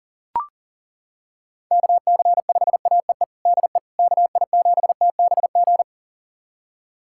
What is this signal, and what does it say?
Morse code, heard as a keyed tone, about 40 wpm, 700 Hz. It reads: KC5AEE DE KI7TBG
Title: USA Callsign DE Callsign - 6 Characters - 40wpm